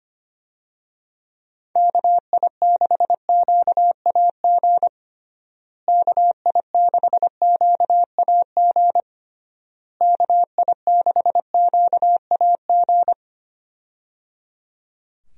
Morse code, heard as a keyed tone, about 25 wpm, 700 Hz. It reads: KI6QAG KI6QAG KI6QAG
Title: USA Call Signs - 6 Characters - Repeated 3x - 25wpm